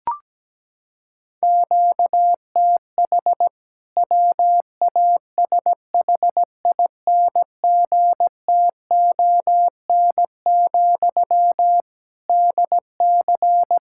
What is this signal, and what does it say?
Morse code, heard as a keyed tone, about 17 wpm, 700 Hz. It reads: QTH WASHINGTON, DC
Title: K1USN SST QSO Practice 17wpm